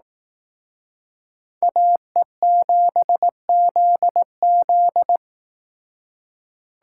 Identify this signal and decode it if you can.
Morse code, heard as a keyed tone, about 18 wpm, 700 Hz. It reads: AE7ZZ
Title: Speed Racing - USA Call Signs 15wpm